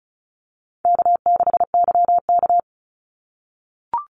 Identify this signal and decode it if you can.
Morse code, heard as a keyed tone, about 34 wpm, 700 Hz. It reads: K6YK
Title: NA SOTA Chaser Callsigns 35wpm